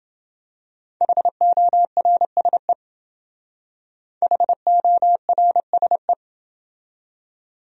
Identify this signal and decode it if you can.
Morse code, heard as a keyed tone, about 28 wpm, 700 Hz. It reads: HORSE HORSE